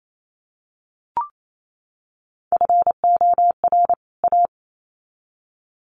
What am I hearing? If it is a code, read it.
Morse code, heard as a keyed tone, about 28 wpm, 700 Hz. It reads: FOR A
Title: Sets of 2 Words 28wpm